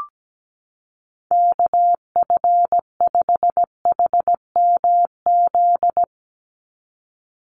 Morse code, heard as a keyed tone, about 17 wpm, 700 Hz. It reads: KF5HMZ
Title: USA Call Signs 17wpm